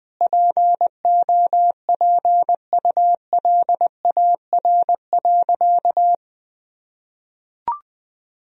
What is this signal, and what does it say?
Morse code, heard as a keyed tone, about 20 wpm, 700 Hz. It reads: POPULAR.
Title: Sentences from Top 1000 Words - Review Part 3 - 20wpm